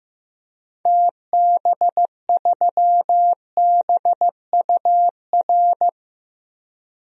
Morse code, heard as a keyed tone, about 15 wpm, 700 Hz. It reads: TB3BUR
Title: DXCC-100 Call Signs - 6 Characters - 15wpm